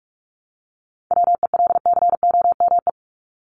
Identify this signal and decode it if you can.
Morse code, heard as a keyed tone, about 44 wpm, 700 Hz. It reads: WELCOME
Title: Top 1751-2000 Words 45wpm